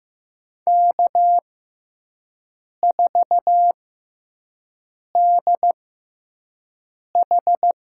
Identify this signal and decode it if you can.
Morse code, heard as a keyed tone, about 15 wpm, spaced as for 5 wpm, 700 Hz. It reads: K4DH
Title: USA Call Signs - 4 Characters - 6x Character Spacing - 15wpm